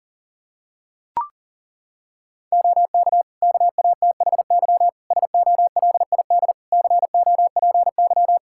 Morse code, heard as a keyed tone, about 40 wpm, 700 Hz. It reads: OK KATHY SOLID COPY
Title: Practice QSO - Exchange 3 - More Info - 40wpm